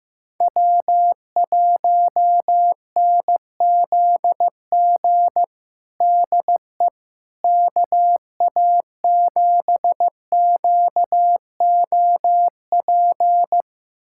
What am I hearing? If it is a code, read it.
Morse code, heard as a keyed tone, about 15 wpm, 700 Hz. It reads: W1NZG DE KA7QOP
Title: USA Callsign DE Callsign 15wpm